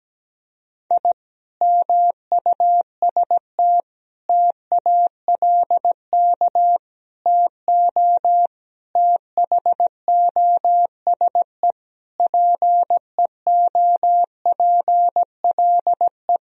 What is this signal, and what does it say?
Morse code, heard as a keyed tone, about 17 wpm, 700 Hz. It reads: I MUST TALK TO THOSE PEOPLE
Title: Sentences from Top 1000 Words - Review Part 1 - 17wpm